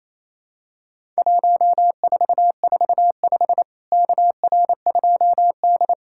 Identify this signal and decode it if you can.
Morse code, heard as a keyed tone, about 28 wpm, 700 Hz. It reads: 1445 KR2D
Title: Worked All Europe DX Contest - QTC Exchange - 28wpm